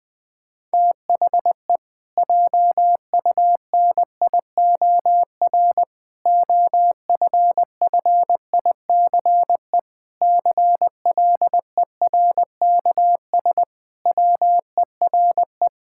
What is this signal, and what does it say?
Morse code, heard as a keyed tone, about 20 wpm, 700 Hz. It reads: THE JUNIOR OFFICE CLERKS WERE
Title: Pangram Challenge v2 - 20wpm